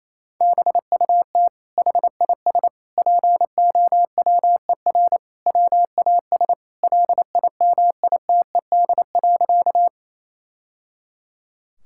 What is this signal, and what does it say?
Morse code, heard as a keyed tone, about 28 wpm, 700 Hz. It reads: BUT HIS POWER WAS LIMITED.
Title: Sentences from Top 1000 Words - Review Part 6 - 28wpm